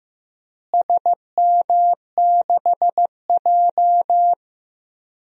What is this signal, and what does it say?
Morse code, heard as a keyed tone, about 15 wpm, 700 Hz. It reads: SM6J
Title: DXCC-100 Call Signs - 4 Characters - Repeated 3x - 15wpm